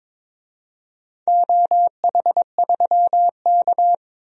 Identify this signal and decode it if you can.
Morse code, heard as a keyed tone, about 22 wpm, 700 Hz. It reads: OH3K
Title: DXCC-100 Call Signs - 4 Characters - 22wpm